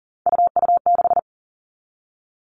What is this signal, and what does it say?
Morse code, heard as a keyed tone, about 40 wpm, 700 Hz. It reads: UU6
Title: All Cut Numbers - 3-Digit 40wpm